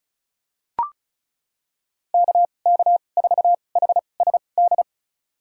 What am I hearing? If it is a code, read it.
Morse code, heard as a keyed tone, about 34 wpm, spaced as for 25 wpm, 700 Hz. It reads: KK4HSD
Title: USA Call Signs - 2x Character Spacing - 35wpm